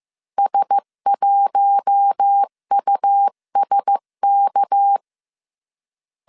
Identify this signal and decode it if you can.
Morse code, heard as a keyed tone, about 15 wpm, 800 Hz. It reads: S1USK
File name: CW / DET / 1 9 1500C S B 14wpm 3302